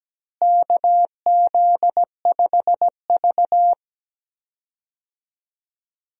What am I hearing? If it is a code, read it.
Morse code, heard as a keyed tone, about 17 wpm, 700 Hz. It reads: KZ5V
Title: USA Call Signs - 4 Characters - Repeated 3x - 17wpm